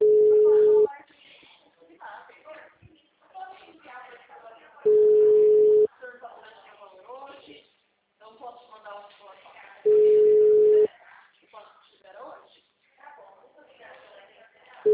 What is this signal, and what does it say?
Ringback tone (Europe, 425 Hz), with speech.
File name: out- -2067- - - .62510